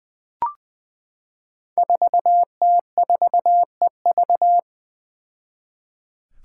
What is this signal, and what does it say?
Morse code, heard as a keyed tone, about 20 wpm, 700 Hz. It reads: 4T4EV
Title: Five Challenging Letter-Number Combinations 20wpm